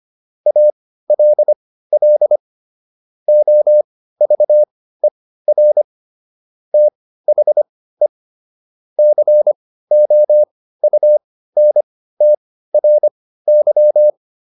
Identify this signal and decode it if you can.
Morse code, heard as a keyed tone, about 25 wpm, spaced as for 15 wpm, 600 Hz. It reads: ALL OVER THE COUNTRY